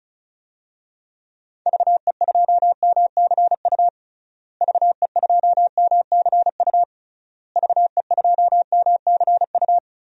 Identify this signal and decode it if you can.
Morse code, heard as a keyed tone, about 36 wpm, 700 Hz. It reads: VE2MCU VE2MCU VE2MCU
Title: DXCC-100 Call Signs - 6 Characters - Repeated 3x - 35wpm